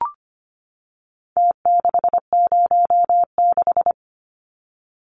Morse code, heard as a keyed tone, about 25 wpm, 700 Hz. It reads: T606
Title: Numbers - Contest Style - 4-Digit 25wpm